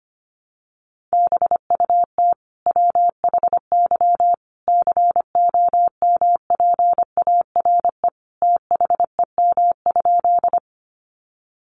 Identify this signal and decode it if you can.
Morse code, heard as a keyed tone, about 25 wpm, 700 Hz. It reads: BUT WHY COMPARE THEM?
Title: Sentences from Top 1300 Words 25wpm